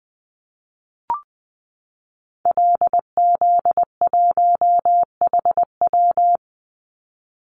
Morse code, heard as a keyed tone, about 20 wpm, 700 Hz. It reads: LZ1HW
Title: Call Signs Repeated 3x - 20wpm